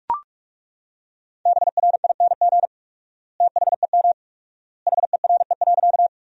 Morse code, heard as a keyed tone, about 41 wpm, 690 Hz. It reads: BRING THEM HERE.
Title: 4x Word-Spacing - Sentences from Top 500 Words - 45wpm